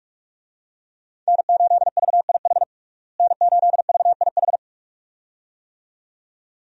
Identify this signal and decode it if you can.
Morse code, heard as a keyed tone, about 46 wpm, 700 Hz. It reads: N8VIH N8VIH
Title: USA Call Signs - 5 Characters - Repeated 2x - 45wpm